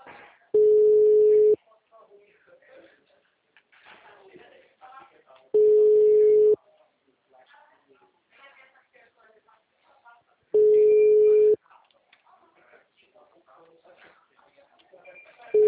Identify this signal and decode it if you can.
Ringback tone (Europe, 425 Hz), with speech.